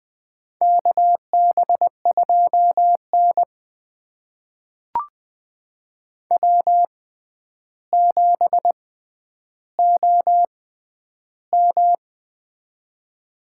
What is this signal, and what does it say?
Morse code, heard as a keyed tone, about 20 wpm, spaced as for 10 wpm, 700 Hz. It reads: KB2N W7OM
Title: USA Call Signs - 4 Characters - 6x Character Spacing - 20wpm